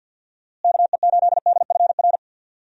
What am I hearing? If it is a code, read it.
Morse code, heard as a keyed tone, about 50 wpm, 700 Hz. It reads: KE8DFR
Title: USA Call Signs - 6 Characters - Repeated 3x - 50wpm